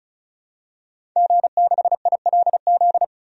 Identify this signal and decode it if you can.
Morse code, heard as a keyed tone, about 34 wpm, 700 Hz. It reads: G6ILZ
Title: DXCC-100 Call Signs - 5 Characters - 35wpm